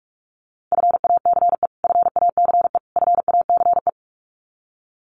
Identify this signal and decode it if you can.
Morse code, heard as a keyed tone, about 44 wpm, 700 Hz. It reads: FACE FACE FACE
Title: Top 500 Words Repeated 45wpm